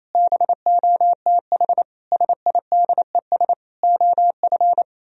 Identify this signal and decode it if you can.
Morse code, heard as a keyed tone, about 37 wpm, 700 Hz. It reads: BOTH SIDES OF